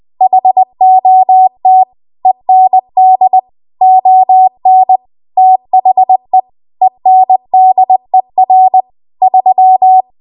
Morse code, heard as a keyed tone, about 20 wpm, 750 Hz. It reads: HOT END ON THE ENDER 3
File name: Archive / 20 WPM / 20WPM